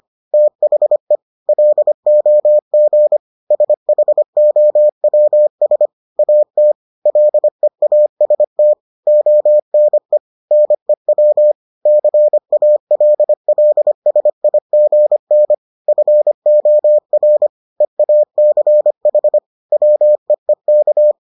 Morse code, heard as a keyed tone, about 25 wpm, 600 Hz. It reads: THE LOG SHOWS AT LEAST ONE NEW CALLSIGN FOR EACH WEEK